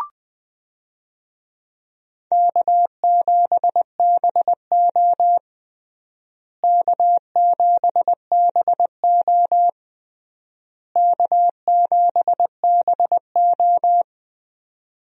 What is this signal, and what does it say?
Morse code, heard as a keyed tone, about 20 wpm, 700 Hz. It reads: K7BO K7BO K7BO
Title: USA Call Signs - 4 Characters - Repeated 3x - 20wpm